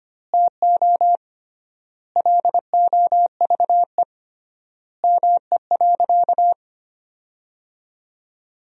Morse code, heard as a keyed tone, about 25 wpm, 700 Hz. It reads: TO LOVE ME.
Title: 3x Word-Spacing - Sentences from Top 400 Words - 25wpm